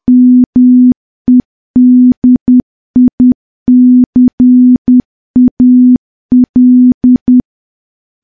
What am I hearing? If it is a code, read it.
Morse code, heard as a keyed tone, about 10 wpm, 250 Hz. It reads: MEDICAL
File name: morse / morse-training-10WPM